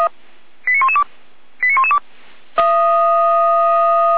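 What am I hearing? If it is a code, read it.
Five-tone selective call (ZVEI), heard as quick runs of five tones, tones of about 70 ms. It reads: 89112 89112
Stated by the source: ZVEI five-tone call 89305 received by SDR, archived by feuerwehr-orlamuende (Orlamünde fire brigade).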